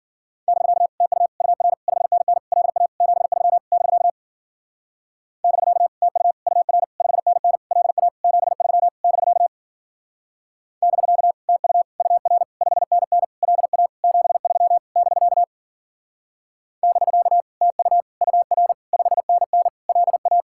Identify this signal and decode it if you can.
Morse code, heard as a keyed tone, about 52 wpm, 700 Hz. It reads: <BK> TU UR 5NN LA 73 <BK> <BK> TU UR 5NN LA 73 <BK> <BK> TU UR 5NN LA 73 <BK> <BK> TU UR 5NN LA